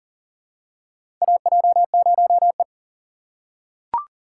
Morse code, heard as a keyed tone, about 40 wpm, 700 Hz. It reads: AJ0E
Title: USA Call Signs - 4 Characters - 6x Character Spacing - 40wpm